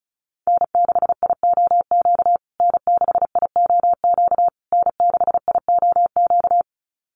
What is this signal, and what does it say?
Morse code, heard as a keyed tone, about 34 wpm, 700 Hz. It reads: N6IOQ N6IOQ N6IOQ
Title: USA Call Signs Repeated 3x 35wpm